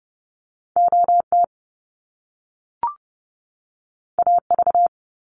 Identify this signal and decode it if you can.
Morse code, heard as a keyed tone, about 30 wpm, 700 Hz. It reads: OT AV
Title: Top Two Letter Combinations in Words 30wpm